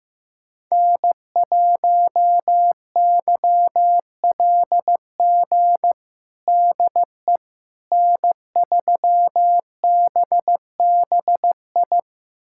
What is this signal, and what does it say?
Morse code, heard as a keyed tone, about 15 wpm, 700 Hz. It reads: N1YLG DE N3BBI
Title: USA Callsign DE Callsign - 5 Characters - 15wpm